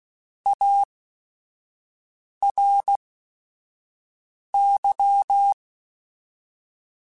Morse code, heard as a keyed tone, about 16 wpm, spaced as for 5 wpm, 800 Hz. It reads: ARY
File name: Technical Subjects / Technical Subjects-mp3s / Sw Pwr Sup @5